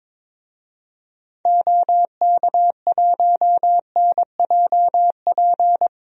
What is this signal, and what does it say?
Morse code, heard as a keyed tone, about 21 wpm, 700 Hz. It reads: OK1NJP
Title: DXCC-100 Call Signs - 6 Characters - Repeated 3x - 22wpm